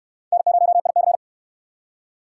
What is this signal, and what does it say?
Morse code, heard as a keyed tone, about 67 wpm, 700 Hz. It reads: N0IC